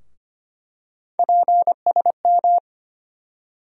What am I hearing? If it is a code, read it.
Morse code, heard as a keyed tone, about 25 wpm, 700 Hz. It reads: PSM
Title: USA Call Sign Postfixes - 3 Letters 25wpm